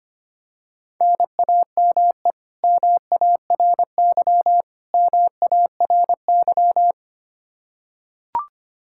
Morse code, heard as a keyed tone, about 25 wpm, 700 Hz. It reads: NAME MARY MARY
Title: K1USN SST QSO Practice 25wpm